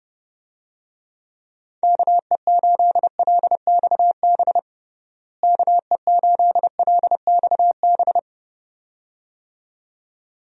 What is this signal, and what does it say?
Morse code, heard as a keyed tone, about 30 wpm, 700 Hz. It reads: KE8LXB KE8LXB
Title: USA Call Signs - 6 Characters - Repeated 2x - 30wpm